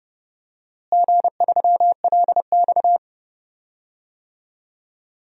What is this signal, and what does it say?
Morse code, heard as a keyed tone, about 30 wpm, 700 Hz. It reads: G3LX
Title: DXCC-100 Call Signs - 4 Characters - Repeated 2x - 30wpm